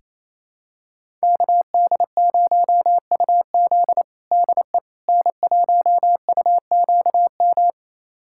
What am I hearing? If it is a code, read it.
Morse code, heard as a keyed tone, about 28 wpm, 700 Hz. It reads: KD0UZ DE N1UQM
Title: USA Callsign DE Callsign - 5 Characters - 28wpm